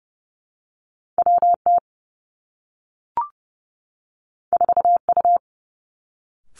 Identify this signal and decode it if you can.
Morse code, heard as a keyed tone, about 30 wpm, 700 Hz. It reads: W T 4 U
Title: Two Challenging Letter-Number Combinations 30wpm